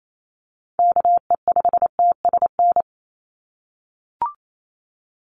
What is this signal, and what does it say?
Morse code, heard as a keyed tone, about 28 wpm, 700 Hz. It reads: KE5TSN
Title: USA Call Signs 28wpm